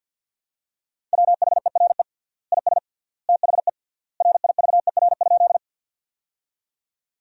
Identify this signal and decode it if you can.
Morse code, heard as a keyed tone, about 52 wpm, 700 Hz. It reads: WHERE IS THE RIVER?
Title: 3x Word-Spacing - Sentences from Top 300 Words - 50wpm